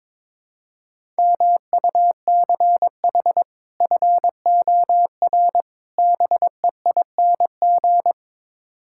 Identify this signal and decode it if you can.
Morse code, heard as a keyed tone, about 22 wpm, 700 Hz. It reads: MUCH FOR BEING
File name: practice / 22wpm / Build A Phrase - 5-Word Phrases - 22wpm